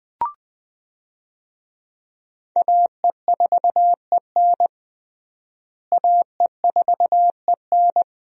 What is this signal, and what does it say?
Morse code, heard as a keyed tone, about 21 wpm, 700 Hz. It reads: AE4EN AE4EN
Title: USA Call Signs - 5 Characters - Repeated 3x - 20wpm